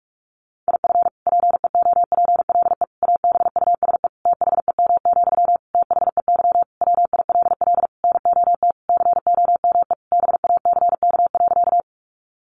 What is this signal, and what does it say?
Morse code, heard as a keyed tone, about 44 wpm, 700 Hz. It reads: IF PEOPLE ABUSE THEM, THEY WILL NOT COME BACK.
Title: Sentences from Top 1300 Words 45wpm